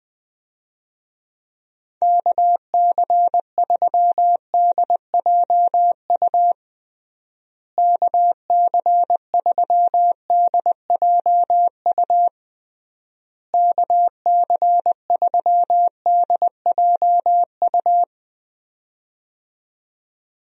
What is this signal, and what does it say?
Morse code, heard as a keyed tone, about 20 wpm, 700 Hz. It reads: KC3DJU KC3DJU KC3DJU
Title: USA Call Signs - 6 Characters - Repeated 3x - 20wpm